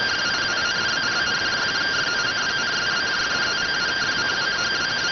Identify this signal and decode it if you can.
Two-tone teleprinter signal, about 170 Hz shift.